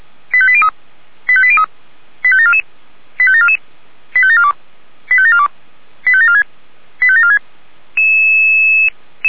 Five-tone selective call (ZVEI), heard as quick runs of five tones, tones of about 70 ms. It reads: 85803 85803 85744 85744 85742 85742 85746 85746